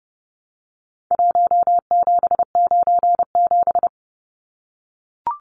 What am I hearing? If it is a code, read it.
Morse code, heard as a keyed tone, about 30 wpm, 700 Hz. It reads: 1797
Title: Numbers - Contest Style - 4-Digit 30wpm